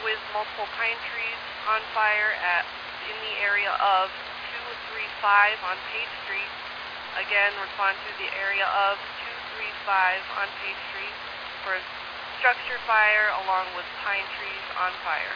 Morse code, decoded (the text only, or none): none